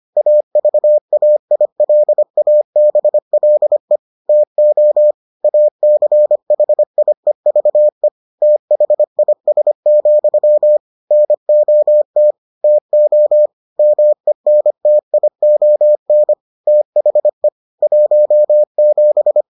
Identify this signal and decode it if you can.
Morse code, heard as a keyed tone, about 25 wpm, 600 Hz. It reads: AVAILABLE TO ACHIEVE THIS, NOT TO MENTION THE 17